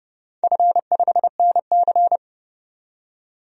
Morse code, heard as a keyed tone, about 31 wpm, 700 Hz. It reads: F5NC
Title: DXCC-100 Call Signs - 4 Characters - 30wpm